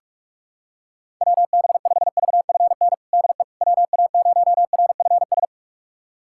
Morse code, heard as a keyed tone, about 42 wpm, 700 Hz. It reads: WB5VFN DE WA0RFS
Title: USA Callsign DE Callsign - 6 Characters - 45wpm